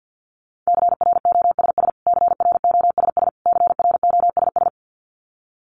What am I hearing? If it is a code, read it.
Morse code, heard as a keyed tone, about 48 wpm, 700 Hz. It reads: CROSS CROSS CROSS
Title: Top 300 Words Repeated 50wpm